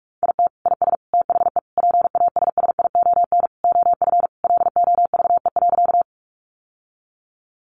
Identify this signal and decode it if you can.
Morse code, heard as a keyed tone, about 44 wpm, 700 Hz. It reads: IT IS THE PASSION OF LOVE.
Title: Sentences from Top 1300 Words 45wpm